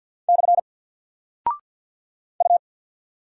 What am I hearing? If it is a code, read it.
Morse code, heard as a keyed tone, about 48 wpm, 700 Hz. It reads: / U
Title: Single Character (Letter:Number:Punctuation:Prosign) Alt 50wpm